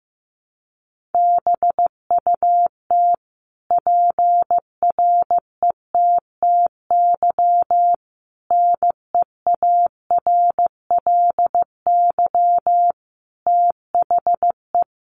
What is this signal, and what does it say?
Morse code, heard as a keyed tone, about 15 wpm, 700 Hz. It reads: BUT PRETTY NEARLY THE
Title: Sentences from Top 1300 Words 15wpm